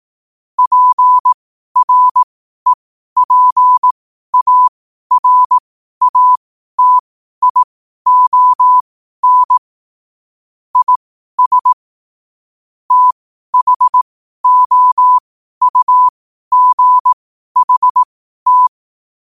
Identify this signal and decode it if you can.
Morse code, heard as a keyed tone, about 18 wpm, spaced as for 12 wpm, 1000 Hz. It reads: PREPARATION IS THOUGHT
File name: qotdfiles / 12wpm